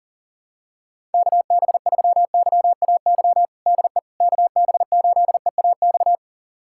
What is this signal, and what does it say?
Morse code, heard as a keyed tone, about 37 wpm, 700 Hz. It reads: KB3YAY DE KB8EAX
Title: USA Callsign DE Callsign - 6 Characters - 40wpm